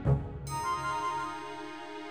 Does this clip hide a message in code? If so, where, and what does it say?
No.